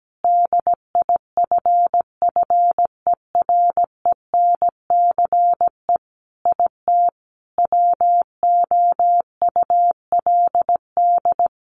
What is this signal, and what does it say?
Morse code, heard as a keyed tone, about 17 wpm, 700 Hz. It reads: DIFFERENCE IT WOULD
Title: Sentences from Top 1300 Words 17wpm